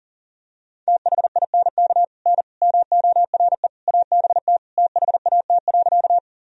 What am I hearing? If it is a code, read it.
Morse code, heard as a keyed tone, about 40 wpm, 700 Hz. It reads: THINK N MORE ABT THAT.
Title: Sentences from Top 200 Words and Abbreviations 40wpm